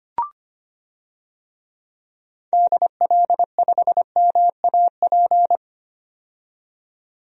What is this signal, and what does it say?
Morse code, heard as a keyed tone, about 25 wpm, 700 Hz. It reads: DL5MAP
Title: DXCC-100 Call Signs - 6 Characters - 25wpm